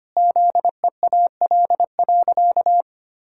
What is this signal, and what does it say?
Morse code, heard as a keyed tone, about 25 wpm, 700 Hz. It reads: ZEAL.
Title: Pangram Challenge v2 - 25wpm